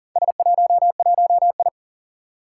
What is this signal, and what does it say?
Morse code, heard as a keyed tone, about 40 wpm, 690 Hz. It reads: S11I
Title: Four Challenging Letter-Number Combinations 40wpm